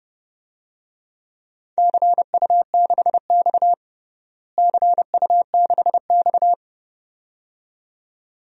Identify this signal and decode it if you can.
Morse code, heard as a keyed tone, about 30 wpm, 700 Hz. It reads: CU6X CU6X
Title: DXCC-100 Call Signs - 4 Characters - Repeated 2x - 30wpm